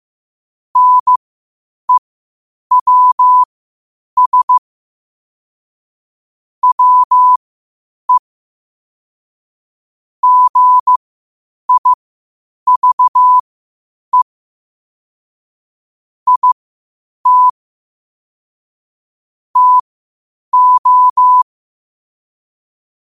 Morse code, heard as a keyed tone, about 15 wpm, spaced as for 8 wpm, 1000 Hz. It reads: NEWS WE GIVE IT TO
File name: qotdfiles / 8wpm